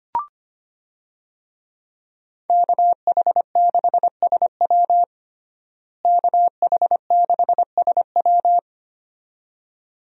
Morse code, heard as a keyed tone, about 25 wpm, 700 Hz. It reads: KH6SW KH6SW
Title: DXCC-100 Call Signs - 5 Characters - Repeated 2x - 25wpm